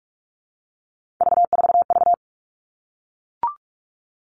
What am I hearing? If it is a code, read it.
Morse code, heard as a keyed tone, about 44 wpm, 700 Hz. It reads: V4V
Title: All Cut Numbers - 3-Digit 45wpm